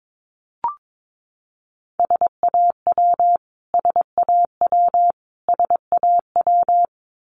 Morse code, heard as a keyed tone, about 22 wpm, 700 Hz. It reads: SAW SAW SAW
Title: Top 500 Words Repeated 22wpm